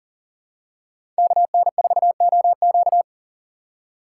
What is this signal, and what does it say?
Morse code, heard as a keyed tone, about 40 wpm, 700 Hz. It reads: KN4OQ
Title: USA Call Signs - 4x Character Spacing - 40wpm